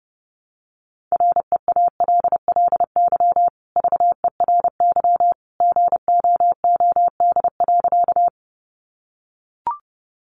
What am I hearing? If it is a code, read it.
Morse code, heard as a keyed tone, about 30 wpm, 700 Hz. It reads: REALLY VERY GOOD.
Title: Sentences from Top 600 Words 30wpm